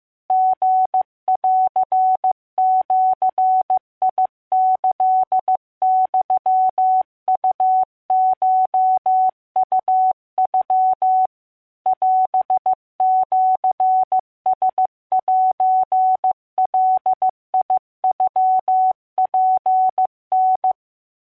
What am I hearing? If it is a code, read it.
Morse code (Japanese Wabun code), heard as a keyed tone, about 15 wpm, 750 Hz. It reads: リンジキユウコウノ オシラセガノツタ